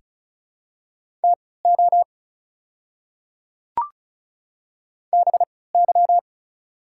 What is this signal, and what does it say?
Morse code, heard as a keyed tone, about 34 wpm, spaced as for 20 wpm, 690 Hz. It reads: TO BY
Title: 3x Character Spacing - Top 100 Words - 35wpm